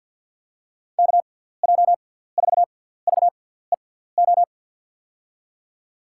Morse code, heard as a keyed tone, about 52 wpm, spaced as for 17 wpm, 700 Hz. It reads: KJ4VEO
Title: USA Call Signs - 6x Character Spacing - 50wpm